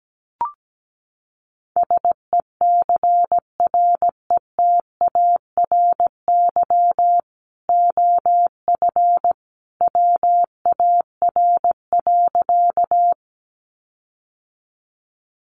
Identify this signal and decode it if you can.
Morse code, heard as a keyed tone, about 17 wpm, 700 Hz. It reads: SECRETARY OF WAR.
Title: Sentences from Top 1300 Words 17wpm